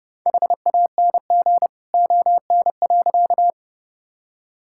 Morse code, heard as a keyed tone, about 32 wpm, 700 Hz. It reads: HANG ON.
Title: Sentences from Top 1600 Words 30wpm